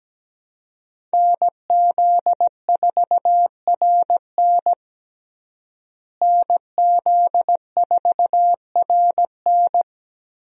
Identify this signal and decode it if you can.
Morse code, heard as a keyed tone, about 17 wpm, 700 Hz. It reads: NZ4RN NZ4RN
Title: USA Call Signs - 5 Characters - Repeated 2x - 17wpm